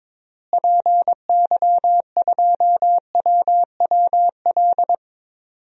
Morse code, heard as a keyed tone, about 22 wpm, 700 Hz. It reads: PY2WWL
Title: DXCC-100 Call Signs - 6 Characters - Repeated 2x - 22wpm